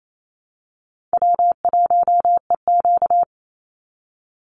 Morse code, heard as a keyed tone, about 27 wpm, 700 Hz. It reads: W1EQ
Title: Easy Call Signs Repeated 3x - 28wpm